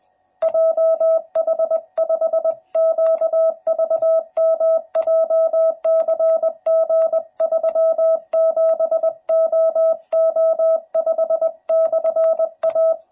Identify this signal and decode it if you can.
Morse code, heard as a keyed tone, about 20 wpm, 650 Hz. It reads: JH5QVM JCG37OO5/A
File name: jh8cbh / 2021cw / cw12b